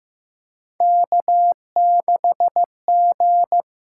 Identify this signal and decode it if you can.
Morse code, heard as a keyed tone, about 15 wpm, 700 Hz. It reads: K6G
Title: USA Call Signs - 6x Character Spacing - 15wpm